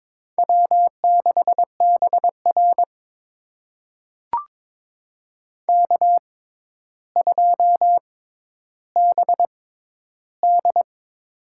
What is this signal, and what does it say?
Morse code, heard as a keyed tone, about 22 wpm, spaced as for 11 wpm, 700 Hz. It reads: W6BR K2BD
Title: USA Call Signs - 6x Character Spacing - 22wpm